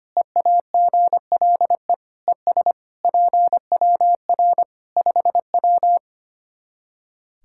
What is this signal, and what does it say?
Morse code, heard as a keyed tone, about 25 wpm, 700 Hz. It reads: EAGLE ES PWR 5W
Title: Practice QSO - Exchange 2 - Info - 25wpm